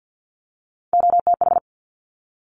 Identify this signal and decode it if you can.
Morse code, heard as a keyed tone, about 50 wpm, 700 Hz. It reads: OTH
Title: Top Three Letter Combinations in Words 50wpm